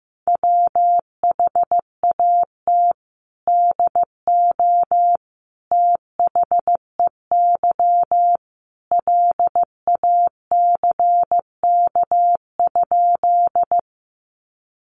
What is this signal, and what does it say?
Morse code, heard as a keyed tone, about 15 wpm, 700 Hz. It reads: WHAT DO THEY LACK?
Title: Sentences from Top 1100 Words 15wpm